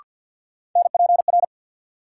Morse code, heard as a keyed tone, about 47 wpm, 700 Hz. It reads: NPR
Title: Sets of 5 Words - New Word Drill - 50wpm